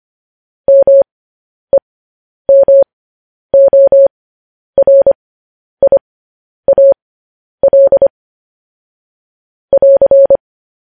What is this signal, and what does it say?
Morse code, heard as a keyed tone, about 25 wpm, spaced as for 10 wpm, 550 Hz. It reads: MEMORIAL <AR>
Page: news-headlines-in-morse-code-at-10-wpm